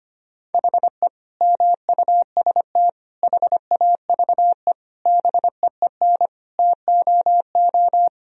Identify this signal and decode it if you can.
Morse code, heard as a keyed tone, about 25 wpm, 700 Hz. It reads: HE MUST HAVE BEEN TOO
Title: Sentences from Top 1000 Words - Review Part 3 - 25wpm